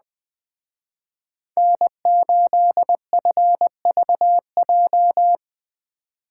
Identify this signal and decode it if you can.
Morse code, heard as a keyed tone, about 19 wpm, 700 Hz. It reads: N8FVJ
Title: Speed Racing - USA Call Signs 17wpm